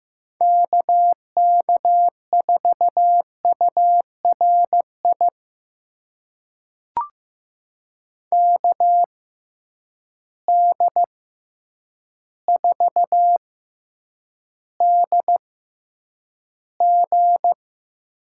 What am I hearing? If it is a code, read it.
Morse code, heard as a keyed tone, about 15 wpm, spaced as for 8 wpm, 700 Hz. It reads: KK4URI KD4DG
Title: USA Call Signs - 6 Characters - 6x Character Spacing - 15wpm